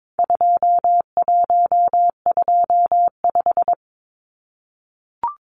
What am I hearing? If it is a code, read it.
Morse code, heard as a keyed tone, about 22 wpm, 700 Hz. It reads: 2125
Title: Numbers - 4-Digit 22wpm